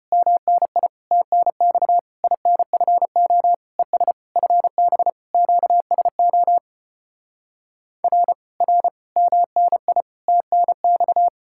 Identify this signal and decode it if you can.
Morse code, heard as a keyed tone, about 33 wpm, 700 Hz. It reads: MNI TNX INFO ES FB QSO R R MNI TNX